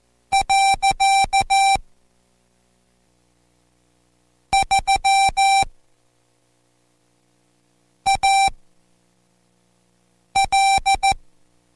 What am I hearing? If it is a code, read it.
Morse code, heard as a keyed tone, about 14 wpm, spaced as for 4 wpm, 780 Hz. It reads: . 3AL